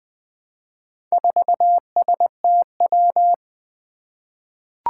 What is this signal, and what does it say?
Morse code, heard as a keyed tone, about 21 wpm, 700 Hz. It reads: 4STW
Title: Four Challenging Letter-Number Combinations 20wpm